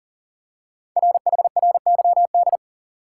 Morse code, heard as a keyed tone, about 41 wpm, 700 Hz. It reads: RHRYD
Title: Code Groups - 5 Letters - 40wpm